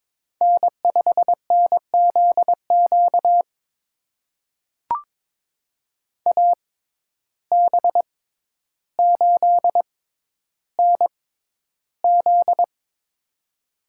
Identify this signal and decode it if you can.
Morse code, heard as a keyed tone, about 22 wpm, spaced as for 11 wpm, 700 Hz. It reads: N5NZQ AB8NZ
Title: USA Call Signs - 5 Characters - 6x Character Spacing - 22wpm